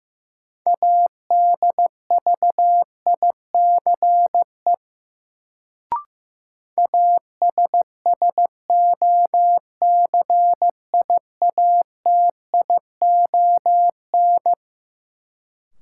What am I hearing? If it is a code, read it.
Morse code, heard as a keyed tone, about 15 wpm, 700 Hz. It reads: ADVICE ASSOCIATION
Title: Sentences from Top 1100 Words - New Word Drill - 15wpm